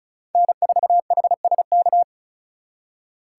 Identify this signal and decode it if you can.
Morse code, heard as a keyed tone, about 45 wpm, 700 Hz. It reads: N4HSK